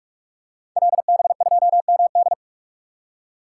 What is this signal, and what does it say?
Morse code, heard as a keyed tone, about 44 wpm, 700 Hz. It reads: LB2MD
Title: Code Groups - 5 Letter:Numbers - 45wpm